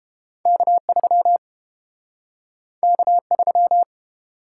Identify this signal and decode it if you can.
Morse code, heard as a keyed tone, about 30 wpm, 700 Hz. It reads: K3 K3